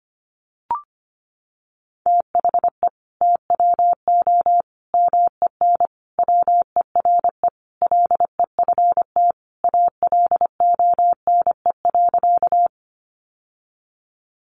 Morse code, heard as a keyed tone, about 25 wpm, 700 Hz. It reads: THE TWO MEN WERE LEFT ALONE.
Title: Sentences from Top 1000 Words 25wpm